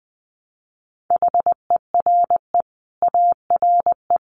Morse code, heard as a keyed tone, about 20 wpm, 700 Hz. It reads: HERE ARE
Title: Sets of 5 Words Encore 20wpm